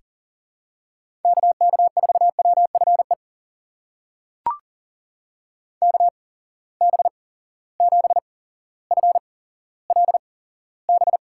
Morse code, heard as a keyed tone, about 39 wpm, spaced as for 17 wpm, 700 Hz. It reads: KK4WFE KB7FLB
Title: USA Call Signs - 6 Characters - 8x Character Spacing - 40wpm